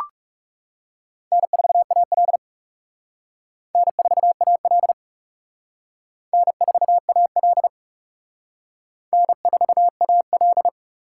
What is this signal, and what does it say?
Morse code, heard as a keyed tone, about 37 wpm, 700 Hz. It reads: N4AL N4AL N4AL N4AL
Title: Speed Racing - USA 1x2 Call Signs 30wpm